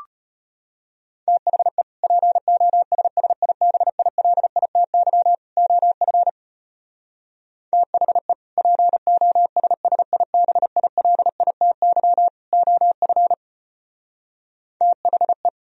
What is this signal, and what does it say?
Morse code, heard as a keyed tone, about 33 wpm, 700 Hz. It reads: THE POSSIBILITY OF THE POSSIBILITY OF THE